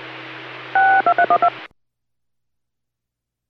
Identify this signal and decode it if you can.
DTMF, heard as telephone touch tones, tones of about 80 ms. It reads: 62312